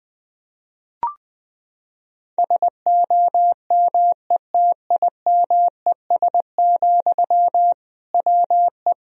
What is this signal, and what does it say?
Morse code, heard as a keyed tone, about 21 wpm, 700 Hz. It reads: SOMETIMES, WE
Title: Sentences from Top 1000 Words - Review Part 3 - 20wpm